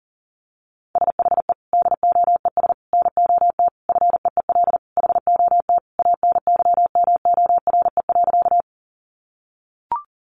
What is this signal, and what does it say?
Morse code, heard as a keyed tone, about 40 wpm, 700 Hz. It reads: SHE DOES NOT FEEL HOT ANYMORE.
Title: Easy Phrases 40wpm